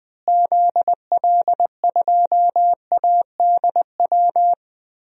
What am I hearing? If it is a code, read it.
Morse code, heard as a keyed tone, about 20 wpm, 700 Hz. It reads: ZL2ADW
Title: DXCC-100 Call Signs - 6 Characters - Repeated 3x - 20wpm